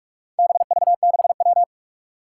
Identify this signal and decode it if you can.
Morse code, heard as a keyed tone, about 44 wpm, 700 Hz. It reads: BV6W
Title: Four Challenging Letter-Number Combinations 45wpm